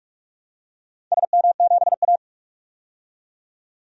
Morse code, heard as a keyed tone, about 43 wpm, 700 Hz. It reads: SM7A
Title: DXCC-100 Call Signs - 4 Characters - 45wpm